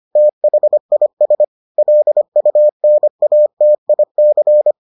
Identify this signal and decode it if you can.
Morse code, heard as a keyed tone, about 25 wpm, 600 Hz. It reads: THIS LUNATIC